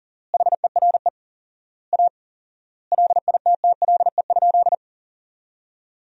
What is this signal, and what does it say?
Morse code, heard as a keyed tone, about 40 wpm, 700 Hz. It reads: HERE A LITTLE?
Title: 4x Word-Spacing - Sentences from Top 500 Words - 40wpm